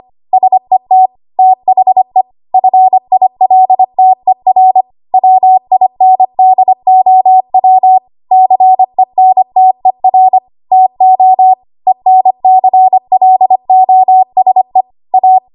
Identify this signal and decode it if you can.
Morse code, heard as a keyed tone, about 25 wpm, 750 Hz. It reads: SET THE FILTER WINDOW CENTER TO ENCLOSE A
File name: Archive / 25 WPM / 25WPM